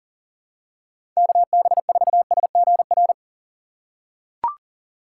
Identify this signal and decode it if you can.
Morse code, heard as a keyed tone, about 38 wpm, 700 Hz. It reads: KB4SGR
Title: USA Call Signs - 6 Characters - 6x Character Spacing - 40wpm